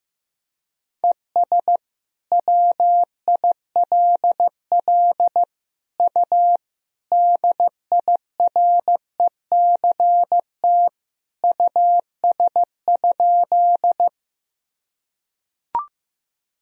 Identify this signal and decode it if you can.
Morse code, heard as a keyed tone, about 16 wpm, 700 Hz. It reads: ES WILL U DIRECT US?
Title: Sentences from Top 400 Words and Abbreviations 15wpm